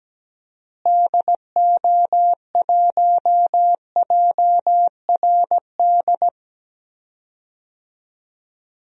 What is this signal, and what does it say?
Morse code, heard as a keyed tone, about 17 wpm, 700 Hz. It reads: DO1JRD
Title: DXCC-100 Call Signs - 6 Characters - Repeated 3x - 17wpm